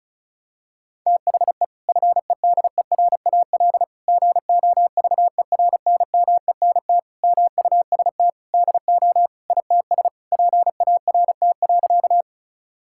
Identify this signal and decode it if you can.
Morse code, heard as a keyed tone, about 36 wpm, 700 Hz. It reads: THE FEDERAL GOVERNMENT MUST DO ITS PART.
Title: Sentences from Top 1000 Words - Review Part 5 - 35wpm